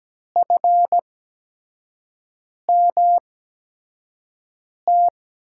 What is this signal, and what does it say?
Morse code, heard as a keyed tone, about 17 wpm, spaced as for 5 wpm, 700 Hz. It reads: FMT
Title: USA Call Signs - 6 Characters - 8x Character Spacing - 17wpm